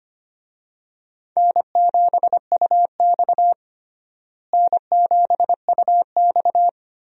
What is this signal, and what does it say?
Morse code, heard as a keyed tone, about 25 wpm, 700 Hz. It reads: N7UX N7UX
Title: USA Call Signs - 4 Characters - Repeated 2x - 25wpm